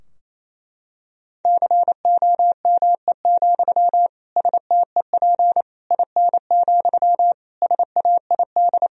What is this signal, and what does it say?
Morse code, heard as a keyed tone, about 28 wpm, 700 Hz. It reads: COME, STEP IN, SAID
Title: Sentences from Top 1000 Words - Review Part 2 - 28wpm